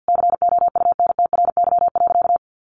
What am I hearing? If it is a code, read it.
Morse code, heard as a keyed tone, about 50 wpm, 700 Hz. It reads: COUNTRY.
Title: Sentences from Top 1200 Words 50wpm